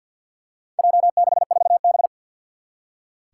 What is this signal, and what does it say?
Morse code, heard as a keyed tone, about 47 wpm, 700 Hz. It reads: J64B
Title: Four Challenging Letter-Number Combinations 50wpm